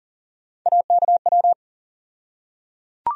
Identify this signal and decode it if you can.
Morse code, heard as a keyed tone, about 40 wpm, 700 Hz. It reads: AKW
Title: USA Call Sign Postfixes - 3 Letters 40wpm